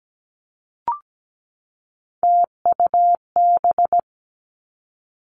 Morse code, heard as a keyed tone, about 17 wpm, 700 Hz. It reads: TUB
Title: Three Challenging Letter-Number Combinations 17wpm